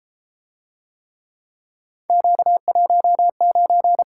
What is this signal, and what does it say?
Morse code, heard as a keyed tone, about 33 wpm, 700 Hz. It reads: Q19